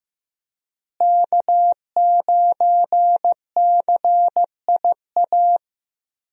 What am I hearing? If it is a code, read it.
Morse code, heard as a keyed tone, about 15 wpm, 700 Hz. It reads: K9CIA
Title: USA Call Signs - 5 Characters - Repeated 3x - 15wpm